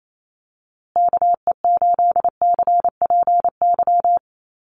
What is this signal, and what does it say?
Morse code, heard as a keyed tone, about 28 wpm, 690 Hz. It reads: KE8CPY
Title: USA Call Signs Repeated 3x 28wpm